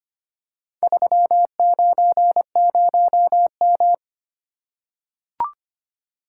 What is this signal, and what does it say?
Morse code, heard as a keyed tone, about 26 wpm, 700 Hz. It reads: 390M
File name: practice / 25wpm / Four Letter-Number Combinations - 25wpm